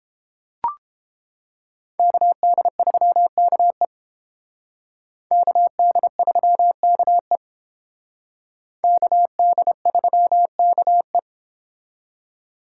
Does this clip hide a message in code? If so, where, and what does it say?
Yes: Morse code from 2.0 to 11.2 s, heard as a keyed tone, about 29 wpm, 700 Hz. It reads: KD3KE KD3KE KD3KE